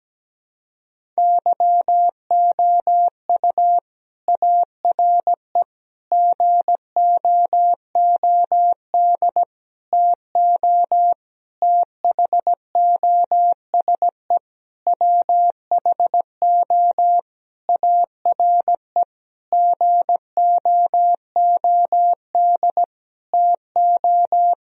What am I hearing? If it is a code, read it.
Morse code, heard as a keyed tone, about 17 wpm, 700 Hz. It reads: YOU ARE GOOD TO THOSE WHO ARE GOOD TO
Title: Sentences from Top 1000 Words - Review Part 1 - 17wpm